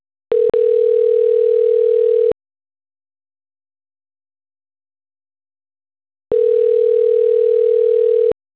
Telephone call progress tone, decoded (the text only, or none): ringback tone